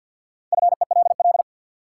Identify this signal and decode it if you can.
Morse code, heard as a keyed tone, about 53 wpm, 700 Hz. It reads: FELL
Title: Sentences from Top 1600 Words - New Word Drill Alt - 50wpm